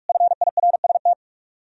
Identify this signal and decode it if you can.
Morse code, heard as a keyed tone, about 57 wpm, 700 Hz. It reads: FIRST